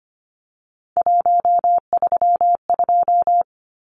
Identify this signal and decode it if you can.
Morse code, heard as a keyed tone, about 25 wpm, 700 Hz. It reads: 132
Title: Numbers - 3-Digit 25wpm